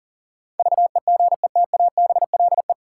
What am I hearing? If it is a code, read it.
Morse code, heard as a keyed tone, about 40 wpm, 700 Hz. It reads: VEGETABLE
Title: Sentences from Top 1300 Words - New Word Drill Alt - 40wpm